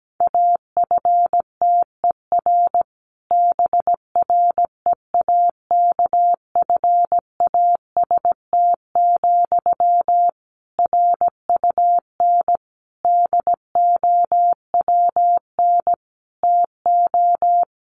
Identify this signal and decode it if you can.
Morse code, heard as a keyed tone, about 17 wpm, 700 Hz. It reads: AFTER BREAKFAST, RUN DOWN TO
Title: Sentences from Top 1500 Words 17wpm